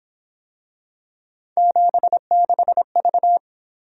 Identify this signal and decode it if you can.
Morse code, heard as a keyed tone, about 27 wpm, 700 Hz. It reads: 76V
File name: Speed-Racing - Three Letter-Number Combinations - 22wpm